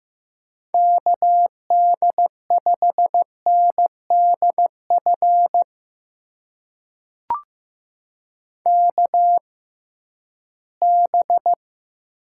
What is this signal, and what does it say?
Morse code, heard as a keyed tone, about 15 wpm, spaced as for 11 wpm, 700 Hz. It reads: KD5NDF KB
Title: USA Call Signs - 6 Characters - 6x Character Spacing - 15wpm